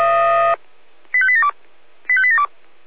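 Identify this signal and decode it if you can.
Five-tone selective call (ZVEI), heard as quick runs of five tones, tones of about 70 ms. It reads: 85872 85872
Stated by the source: ZVEI five-tone call received by SDR.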